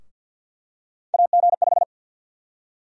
Morse code, heard as a keyed tone, about 48 wpm, 700 Hz. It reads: AG5
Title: USA Call Sign Prefixes - Repeated 2x - 50wpm